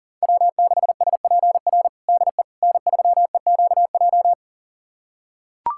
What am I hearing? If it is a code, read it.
Morse code, heard as a keyed tone, about 40 wpm, 700 Hz. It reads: W6SPR DE N3EQJ
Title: USA Callsign DE Callsign - 5 Characters - 40wpm